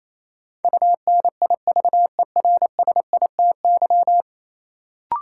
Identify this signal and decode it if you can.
Morse code, heard as a keyed tone, about 28 wpm, 700 Hz. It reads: UNIVERSITY
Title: Sentences from Top 1000 Words - Review Part 4 - New Word Drill - 28wpm